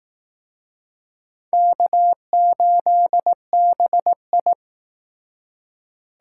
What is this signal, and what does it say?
Morse code, heard as a keyed tone, about 18 wpm, 700 Hz. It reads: K8BI
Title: Speed Racing - USA 1x2 Call Signs 15wpm